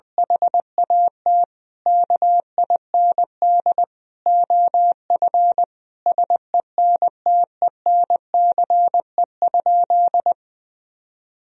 Morse code, heard as a keyed tone, about 21 wpm, 700 Hz. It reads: HAT KIND OF SENTENCE?
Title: Sentences from Top 1000 Words - Review Part 1 - 20wpm